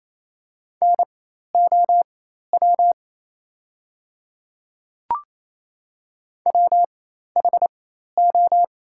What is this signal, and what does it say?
Morse code, heard as a keyed tone, about 27 wpm, spaced as for 13 wpm, 700 Hz. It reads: NOW WHO
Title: 4x Character Spacing - Top 100 Words - 28wpm